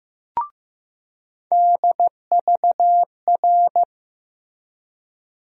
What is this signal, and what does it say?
Morse code, heard as a keyed tone, about 15 wpm, 700 Hz. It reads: DVR
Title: USA Call Sign Postfixes - 3 Letters 15wpm